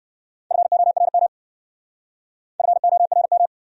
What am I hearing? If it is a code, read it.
Morse code, heard as a keyed tone, about 54 wpm, 700 Hz. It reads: FORM FORM
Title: Speed-Racing Top 200 Words 45wpm